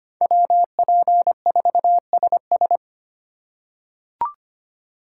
Morse code, heard as a keyed tone, about 25 wpm, 700 Hz. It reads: WP4SS
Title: USA Call Signs - 3x Character Spacing - 25wpm